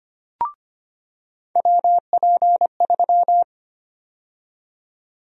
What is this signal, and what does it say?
Morse code, heard as a keyed tone, about 25 wpm, 700 Hz. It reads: WP3
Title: USA Call Sign Prefixes - 25wpm